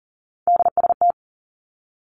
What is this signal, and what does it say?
Morse code, heard as a keyed tone, about 38 wpm, 700 Hz. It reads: DST
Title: Three Challenging Letter-Number Combinations 40wpm